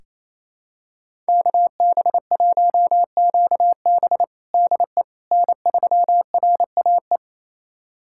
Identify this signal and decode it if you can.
Morse code, heard as a keyed tone, about 28 wpm, 700 Hz. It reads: KB1QB DE N3RAE
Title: USA Callsign DE Callsign - 5 Characters - 28wpm